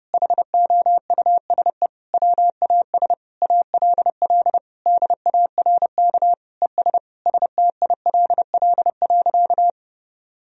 Morse code, heard as a keyed tone, about 32 wpm, 700 Hz. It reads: HOUSE WAS ALL DARK ES STILL.
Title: Sentences from Top 500 Words and Abbreviations 30wpm